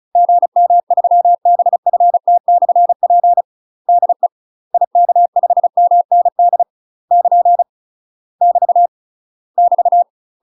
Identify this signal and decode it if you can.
Morse code, heard as a keyed tone, about 36 wpm, 700 Hz. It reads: GM3BFT/P DE IK5MND <KN> <BT> <BT>
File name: UK MASTER QSOs TRAINING / 35WPM / UK MASTER QSOs 0036